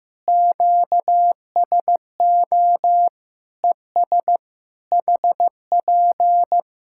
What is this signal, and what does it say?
Morse code, heard as a keyed tone, about 16 wpm, 700 Hz. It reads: QSO ES HP
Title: Practice QSO - Exchange 4 - Ending - 15wpm